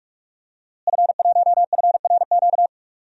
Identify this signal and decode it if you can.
Morse code, heard as a keyed tone, about 42 wpm, 700 Hz. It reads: F1FRQ
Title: DXCC-100 Call Signs - 5 Characters - Repeated 3x - 45wpm